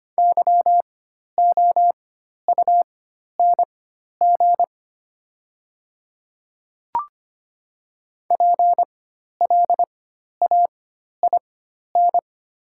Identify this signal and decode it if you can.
Morse code, heard as a keyed tone, about 25 wpm, spaced as for 12 wpm, 700 Hz. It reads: YOUNG PLAIN
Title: 4x Character Spacing - Top 500 Words - 25wpm